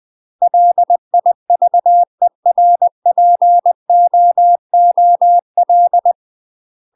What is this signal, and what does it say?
Morse code, heard as a keyed tone, about 20 wpm, 700 Hz. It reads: LIVERPOOL
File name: UK MASTER QSOs TRAINING / 20WPM / UK MASTER QSOs 0029